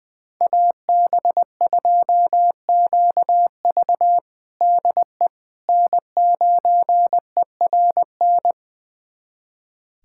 Morse code, heard as a keyed tone, about 21 wpm, 700 Hz. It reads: AB2QV DE N9ERN
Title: USA Callsign DE Callsign - 5 Characters - 20wpm